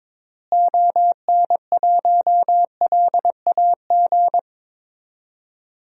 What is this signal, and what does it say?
Morse code, heard as a keyed tone, about 22 wpm, 700 Hz. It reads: ON1LAG
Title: DXCC-100 Call Signs - 6 Characters - Repeated 3x - 22wpm